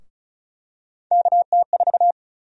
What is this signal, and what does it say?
Morse code, heard as a keyed tone, about 34 wpm, 700 Hz. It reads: KT4
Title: USA Call Sign Prefixes - 35wpm